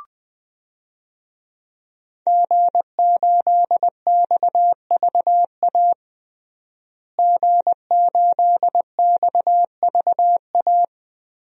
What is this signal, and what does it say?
Morse code, heard as a keyed tone, about 20 wpm, 700 Hz. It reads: G8XVA G8XVA
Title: DXCC-100 Call Signs - 5 Characters - Repeated 2x - 20wpm